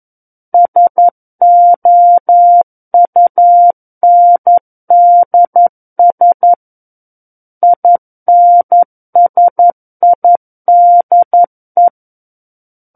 Morse code, heard as a keyed tone, about 11 wpm, 700 Hz. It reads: SOUNDS INSIDE
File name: Book One / 02-Chapter 2 - 11 WPM